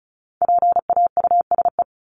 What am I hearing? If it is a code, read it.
Morse code, heard as a keyed tone, about 34 wpm, 700 Hz. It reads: PAUSE
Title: Top 1501-1750 Words 35wpm